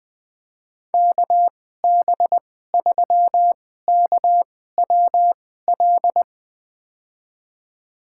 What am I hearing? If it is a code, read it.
Morse code, heard as a keyed tone, about 20 wpm, spaced as for 14 wpm, 700 Hz. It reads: KB3KWL
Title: USA Call Signs - 6 Characters - 2x Character Spacing - 20wpm